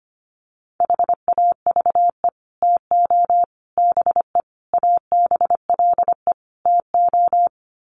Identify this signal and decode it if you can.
Morse code, heard as a keyed tone, about 25 wpm, 700 Hz. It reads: HAVE TO BE ABLE TO
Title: Sets of 5 Words 25wpm